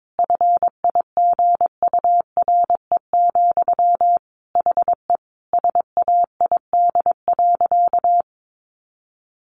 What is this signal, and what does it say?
Morse code, heard as a keyed tone, about 22 wpm, 700 Hz. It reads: FIGURE, HE SAID.
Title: Sentences from Top 1400 Words 22wpm